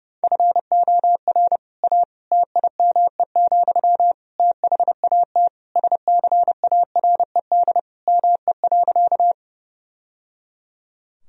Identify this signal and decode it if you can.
Morse code, heard as a keyed tone, about 30 wpm, 700 Hz. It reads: FOR A TIME, THAT SCARED ME.
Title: Sentences from Top 1000 Words - Review Part 4 - 30wpm